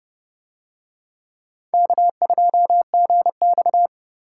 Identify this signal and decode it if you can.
Morse code, heard as a keyed tone, about 29 wpm, 700 Hz. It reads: K2GX